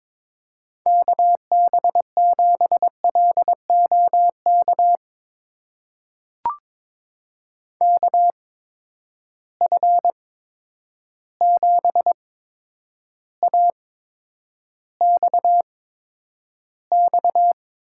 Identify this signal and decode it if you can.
Morse code, heard as a keyed tone, about 22 wpm, spaced as for 9 wpm, 700 Hz. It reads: KB7LOK KF7AXX
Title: USA Call Signs - 6 Characters - 8x Character Spacing - 22wpm